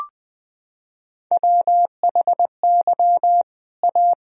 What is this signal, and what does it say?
Morse code, heard as a keyed tone, about 20 wpm, 700 Hz. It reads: WHY A
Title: Sentences from Top 1000 Words - Review Part 2 - 20wpm